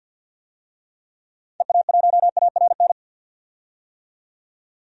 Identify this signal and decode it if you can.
Morse code, heard as a keyed tone, about 51 wpm, 700 Hz. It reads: EA1ARN
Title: DXCC-100 Call Signs - 6 Characters - 50wpm